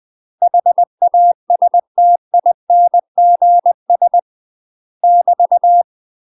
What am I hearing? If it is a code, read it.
Morse code, heard as a keyed tone, about 20 wpm, 700 Hz. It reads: HASTINGS <BT>
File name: UK MASTER QSOs TRAINING/20WPM